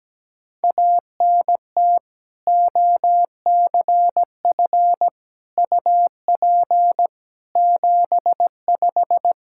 Morse code, heard as a keyed tone, about 17 wpm, 700 Hz. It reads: ANT OCF UP 75
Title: Practice QSO - Exchange 2 - Info - 17wpm